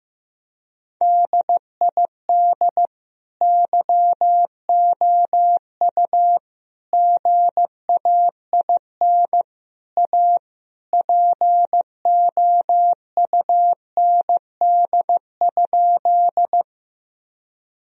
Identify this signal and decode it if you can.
Morse code, heard as a keyed tone, about 15 wpm, 700 Hz. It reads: DID YOU GAIN A POUND?
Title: Sentences from Top 1000 Words - Review Part 2 - 15wpm